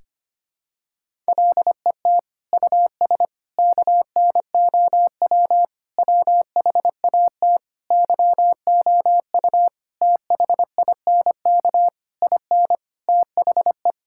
Morse code, heard as a keyed tone, about 25 wpm, 700 Hz. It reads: LET US KNOW WHAT YOU THINK IN THE
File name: Sets of 9 Words - 25wpm